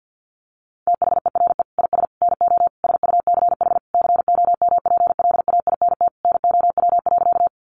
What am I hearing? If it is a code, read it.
Morse code, heard as a keyed tone, about 50 wpm, 700 Hz. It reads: THERE IS NO SUCH COMPLAINT NOW.
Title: Sentences from Top 1400 Words 50wpm